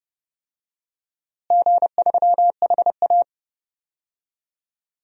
Morse code, heard as a keyed tone, about 28 wpm, 700 Hz. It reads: G3HA
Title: DXCC-100 Call Signs - 4 Characters - Repeated 3x - 30wpm